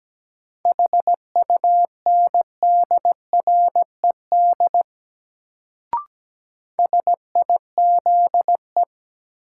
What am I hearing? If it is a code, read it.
Morse code, heard as a keyed tone, about 17 wpm, 700 Hz. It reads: HUNDRED SIZE
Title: Sentences from Top 400 Words - New Word Drill - 17wpm